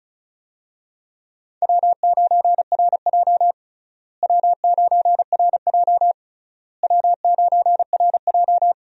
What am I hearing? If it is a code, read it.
Morse code, heard as a keyed tone, about 36 wpm, 700 Hz. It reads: W9RJ W9RJ W9RJ
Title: USA Call Signs - 4 Characters - Repeated 3x - 35wpm